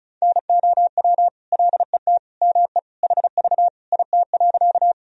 Morse code, heard as a keyed tone, about 36 wpm, 700 Hz. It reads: NOW LET ME HV IT.
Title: Sentences from Top 300 Words and Abbreviations 35wpm